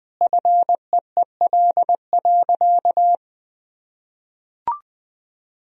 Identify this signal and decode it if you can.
Morse code, heard as a keyed tone, about 20 wpm, 700 Hz. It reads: FEEL.
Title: Sentences from Top 1000 Words - Review Part 3 - 20wpm